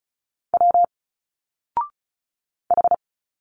Morse code, heard as a keyed tone, about 36 wpm, 700 Hz. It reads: W H
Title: Single Letters 35wpm